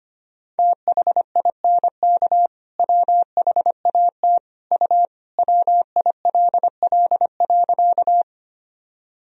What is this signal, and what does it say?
Morse code, heard as a keyed tone, about 25 wpm, 700 Hz. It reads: THINK WHAT U WILL.
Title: Sentences from Top 200 Words and Abbreviations 25wpm